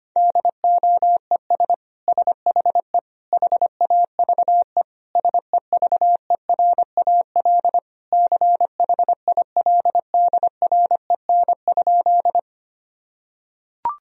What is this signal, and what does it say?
Morse code, heard as a keyed tone, about 25 wpm, 700 Hz. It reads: DOES SHE HAVE SEVERAL CHILDREN?
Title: Sentences from Top 1000 Words - Review Part 2 - 25wpm